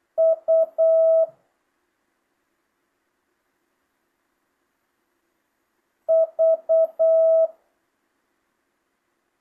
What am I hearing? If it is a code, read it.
Morse code, heard as a keyed tone, about 8 wpm, 650 Hz. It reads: U V
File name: jh8cbh / 2025cw